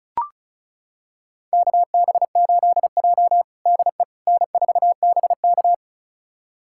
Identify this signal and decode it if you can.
Morse code, heard as a keyed tone, about 36 wpm, 700 Hz. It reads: KB8J DE N4BK
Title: USA Callsign DE Callsign - 4 Characters - 35wpm